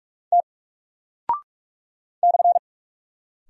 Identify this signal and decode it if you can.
Morse code, heard as a keyed tone, about 44 wpm, 700 Hz. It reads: T /
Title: Single Character (Letter:Number:Punctuation:Prosign) Alt 45wpm